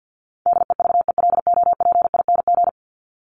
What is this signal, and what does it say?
Morse code, heard as a keyed tone, about 50 wpm, 700 Hz. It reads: DEVELOPING
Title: ING Suffix - Top 100 Words with Base Words 50wpm